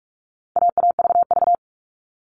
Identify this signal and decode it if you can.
Morse code, heard as a keyed tone, about 44 wpm, 700 Hz. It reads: AAVV
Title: All Cut Numbers - 4-Digit 45wpm